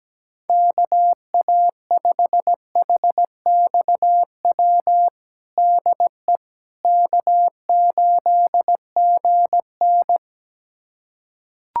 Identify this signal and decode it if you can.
Morse code, heard as a keyed tone, about 17 wpm, 700 Hz. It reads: KA5HXW DE K8GN
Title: USA Callsign DE Callsign 17wpm